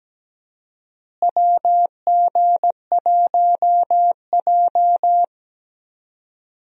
Morse code, heard as a keyed tone, about 17 wpm, 700 Hz. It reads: WG1J
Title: USA Call Signs - 4 Characters - Repeated 3x - 17wpm